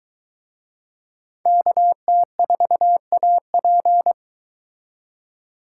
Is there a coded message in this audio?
Yes: Morse code from 1.5 to 4.1 s, heard as a keyed tone, about 22 wpm, 700 Hz. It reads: KT4AP